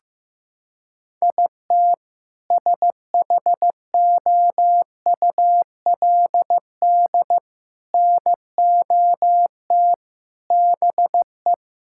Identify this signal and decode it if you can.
Morse code, heard as a keyed tone, about 15 wpm, 700 Hz. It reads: IT SHOULD NOT BE
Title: Sentences from Top 1600 Words 15wpm